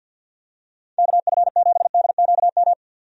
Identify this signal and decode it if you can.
Morse code, heard as a keyed tone, about 52 wpm, 700 Hz. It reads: KF7DQM
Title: USA Call Signs - 6 Characters - 50wpm